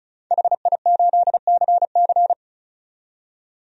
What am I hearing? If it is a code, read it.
Morse code, heard as a keyed tone, about 36 wpm, 700 Hz. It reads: HI8CC
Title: DXCC-100 Call Signs - 5 Characters - 35wpm